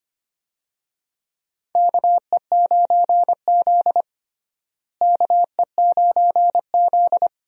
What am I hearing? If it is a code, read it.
Morse code, heard as a keyed tone, about 25 wpm, 700 Hz. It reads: KE9Z KE9Z
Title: USA Call Signs - 4 Characters - Repeated 2x - 25wpm